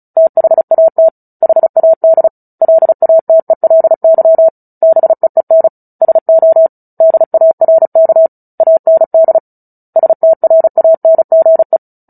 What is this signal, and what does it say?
Morse code, heard as a keyed tone, about 34 wpm, 650 Hz. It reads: THAT HAD LATELY BEEN SO DARK AND STRANGE
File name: Book Two / 09-Chapter 9 - 35 WPM